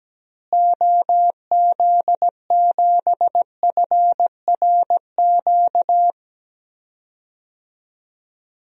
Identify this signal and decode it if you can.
Morse code, heard as a keyed tone, about 17 wpm, 700 Hz. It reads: OZ7FRQ
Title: DXCC-100 Call Signs - 6 Characters - Repeated 3x - 17wpm